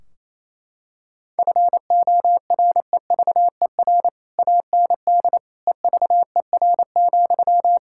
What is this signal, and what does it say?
Morse code, heard as a keyed tone, about 28 wpm, 700 Hz. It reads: FOREVER AND EVER,
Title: Sentences from Top 1600 Words 28wpm